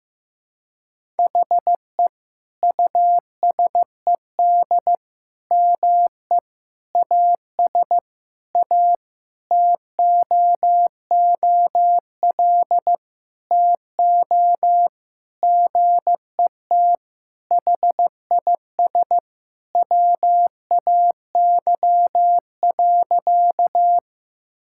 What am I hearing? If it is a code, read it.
Morse code, heard as a keyed tone, about 15 wpm, 700 Hz. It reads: HE USED ME AS A TOOL TO GET HIS WAY.
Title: Sentences from Top 1000 Words - Review Part 6 - 15wpm